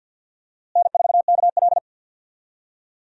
Morse code, heard as a keyed tone, about 48 wpm, 700 Hz. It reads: N4KL
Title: USA Call Signs - 4 Characters - Repeated 2x - 50wpm